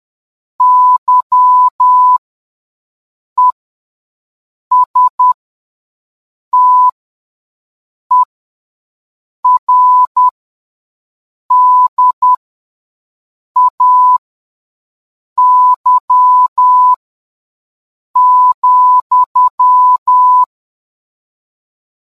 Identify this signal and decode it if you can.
Morse code, heard as a keyed tone, about 10 wpm, spaced as for 5 wpm, 1000 Hz. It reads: YESTERDAY,